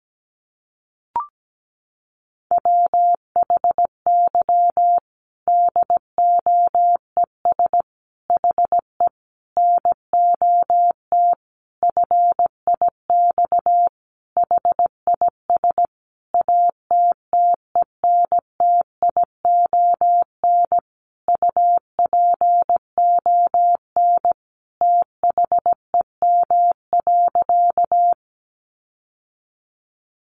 Morse code, heard as a keyed tone, about 17 wpm, 700 Hz. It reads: WHY DOES HE NOT FIX HIS ATTENTION UPON THEM.
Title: Sentences from Top 1400 Words 17wpm